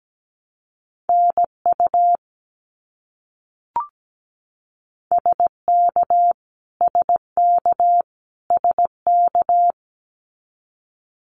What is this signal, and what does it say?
Morse code, heard as a keyed tone, about 17 wpm, 700 Hz. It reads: NU SK SK SK
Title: Canadian Provinces and Territories - Repeated 3x - 17wpm